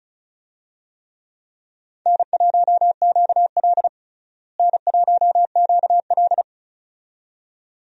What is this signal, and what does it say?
Morse code, heard as a keyed tone, about 36 wpm, 700 Hz. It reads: N1QL N1QL
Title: USA Call Signs - 4 Characters - Repeated 2x - 35wpm